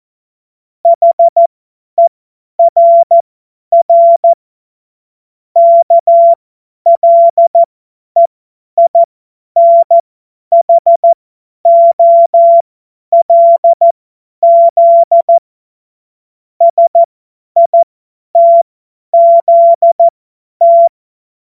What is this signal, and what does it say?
Morse code, heard as a keyed tone, about 14 wpm, spaced as for 10 wpm, 680 Hz. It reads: HERR KLEINHOLZ SITZT
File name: Morsekurs / Witze / Witz 05 März 2024 50 F70